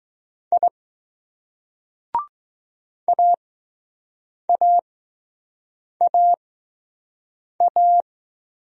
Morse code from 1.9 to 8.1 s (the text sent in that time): AAAA